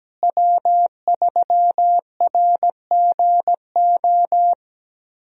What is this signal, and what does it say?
Morse code, heard as a keyed tone, about 17 wpm, 700 Hz. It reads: W3RGO
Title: USA Call Signs - 5 Characters - Repeated 3x - 17wpm